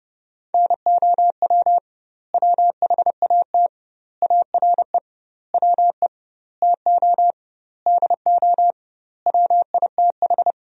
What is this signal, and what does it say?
Morse code, heard as a keyed tone, about 30 wpm, 700 Hz. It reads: NOW WHAT ARE WE TO DO WITH
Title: 2x Word-Spacing - Sentences from Top 100 Words - 30wpm